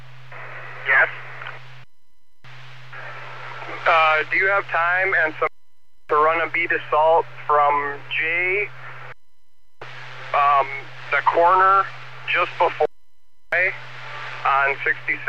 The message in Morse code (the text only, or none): none